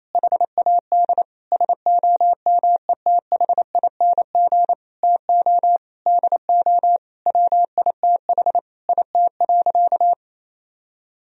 Morse code, heard as a keyed tone, about 28 wpm, 700 Hz. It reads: HAD SOMETHING TO DO WITH IT.
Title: Sentences from Top 1000 Words - Review Part 2 - 28wpm